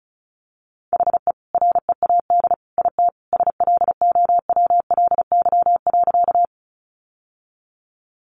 Morse code, heard as a keyed tone, about 34 wpm, 700 Hz. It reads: HE READ IT SLOWLY.
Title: Sentences from Top 1100 Words 35wpm